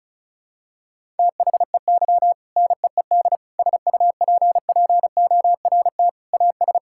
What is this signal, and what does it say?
Morse code, heard as a keyed tone, about 34 wpm, 700 Hz. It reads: THEY NEED SUPPORT AS
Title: Sentences from Top 1000 Words - Review Part 5 - 35wpm